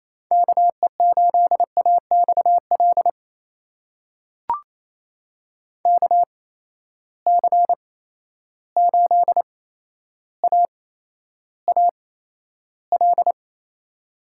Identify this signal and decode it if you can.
Morse code, heard as a keyed tone, about 28 wpm, spaced as for 12 wpm, 700 Hz. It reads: KE8AXL KC8AAL
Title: USA Call Signs - 6 Characters - 8x Character Spacing - 28wpm